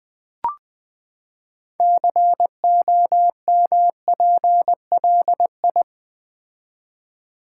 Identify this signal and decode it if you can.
Morse code, heard as a keyed tone, about 20 wpm, 700 Hz. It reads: COMPLI
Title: Build A Word - 6 Letters At A Time - 12-Letter Words - 20wpm